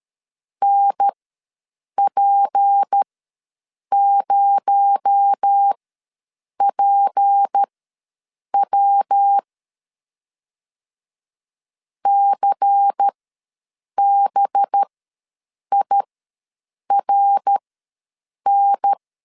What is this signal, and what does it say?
Morse code, heard as a keyed tone, about 13 wpm, spaced as for 7 wpm, 800 Hz. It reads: NP0PW CBIRN
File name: LEZIONI / 6 PKX0 / 6 PKX0 ws 12wpm 3448